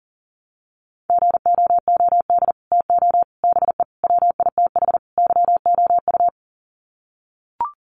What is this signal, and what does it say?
Morse code, heard as a keyed tone, about 37 wpm, 700 Hz. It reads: GOOD TO BE WITH YOU
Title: Sets of 5 Words Encore2 40wpm